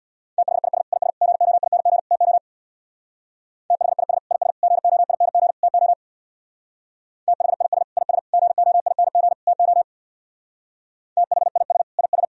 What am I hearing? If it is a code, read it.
Morse code, heard as a keyed tone, about 62 wpm, 700 Hz. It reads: THIS IS GOING TO THIS IS GOING TO THIS IS GOING TO THIS IS
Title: Speed-Racing - Sets of 4 Words 50wpm